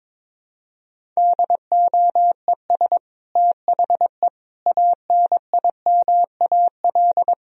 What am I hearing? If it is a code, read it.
Morse code, heard as a keyed tone, about 22 wpm, 700 Hz. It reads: DOES THE ANIMAL
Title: Sentences from Top 1000 Words - Review Part 1 - 22wpm